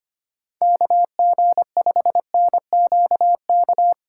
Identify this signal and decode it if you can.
Morse code, heard as a keyed tone, about 25 wpm, 700 Hz. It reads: KG5NQK
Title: USA Call Signs - 6 Characters - 4x Character Spacing - 25wpm